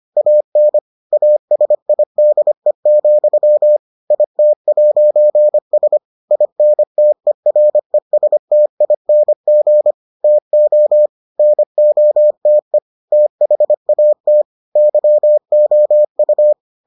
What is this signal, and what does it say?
Morse code, heard as a keyed tone, about 25 wpm, 600 Hz. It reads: AN ASIDE, IT'S INTERESTING TO NOTE THAT YOU